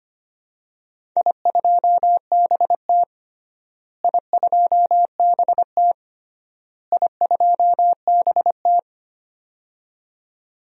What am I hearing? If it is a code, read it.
Morse code, heard as a keyed tone, about 25 wpm, 700 Hz. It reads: I2BT I2BT I2BT
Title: DXCC-100 Call Signs - 4 Characters - Repeated 3x - 25wpm